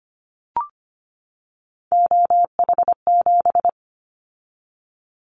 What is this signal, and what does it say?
Morse code, heard as a keyed tone, about 25 wpm, 700 Hz. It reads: OH7
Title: Callsign Prefixes for Top 50 DXCC - 25wpm